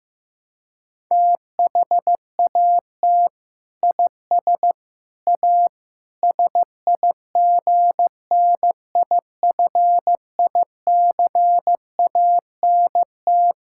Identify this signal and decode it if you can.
Morse code, heard as a keyed tone, about 15 wpm, 700 Hz. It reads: THAT IS A SIGNIFICANT
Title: Sentences from Top 1000 Words - Review Part 6 - 15wpm